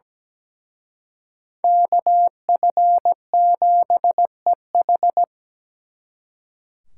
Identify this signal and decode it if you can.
Morse code, heard as a keyed tone, about 20 wpm, 700 Hz. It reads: KF7EH
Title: Speed Racing - USA 2x2 Call Signs 17wpm